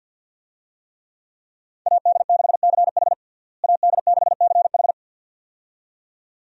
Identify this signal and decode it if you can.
Morse code, heard as a keyed tone, about 53 wpm, 700 Hz. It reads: AD6CH AD6CH
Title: USA Call Signs - 5 Characters - Repeated 2x - 50wpm